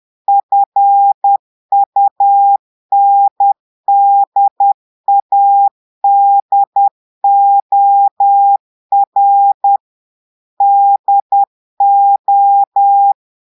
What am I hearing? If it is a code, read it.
Morse code, heard as a keyed tone, about 10 wpm, 800 Hz. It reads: FUNDADOR DO
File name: Portugais - texte en clair 1 - vitesse - 10 mots minute